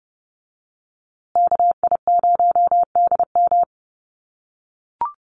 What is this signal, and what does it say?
Morse code, heard as a keyed tone, about 30 wpm, 700 Hz. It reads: KI0DM
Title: USA Call Signs Repeated 3x 30wpm